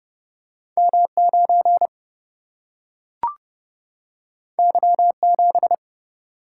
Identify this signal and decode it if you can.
Morse code, heard as a keyed tone, about 31 wpm, 700 Hz. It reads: M9 Y7
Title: Single-Letter Single-Number 30wpm